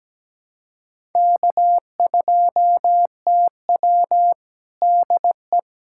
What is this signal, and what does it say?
Morse code, heard as a keyed tone, about 17 wpm, 700 Hz. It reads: K2TW DE
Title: USA Callsign DE Callsign - 4 Characters - 17wpm